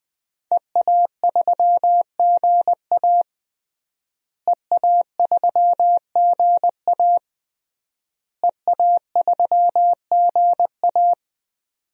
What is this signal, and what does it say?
Morse code, heard as a keyed tone, about 20 wpm, 700 Hz. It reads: EA3GA EA3GA EA3GA
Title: DXCC-100 Call Signs - 5 Characters - Repeated 3x - 20wpm